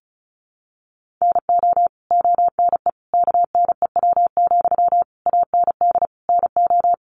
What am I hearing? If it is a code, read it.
Morse code, heard as a keyed tone, about 34 wpm, 700 Hz. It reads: NO ONE KNEW, AND NO
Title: Sentences from Top 1400 Words 35wpm